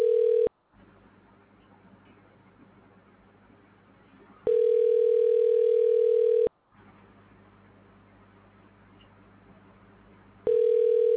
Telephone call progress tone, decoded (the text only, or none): ringback tone